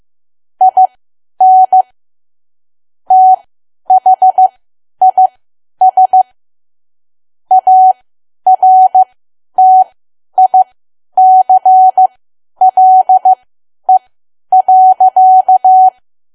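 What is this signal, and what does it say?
Morse code, heard as a keyed tone, about 15 wpm, spaced as for 10 wpm, 750 Hz. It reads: IN THIS ARTICLE.
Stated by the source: Morse learning practice at 10 wpm.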